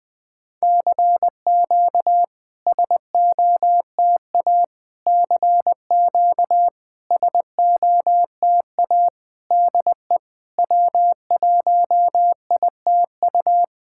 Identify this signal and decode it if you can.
Morse code, heard as a keyed tone, about 21 wpm, 700 Hz. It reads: CQ SOTA CQ SOTA DE W1ITU
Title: SOTA QSO Practice 20wpm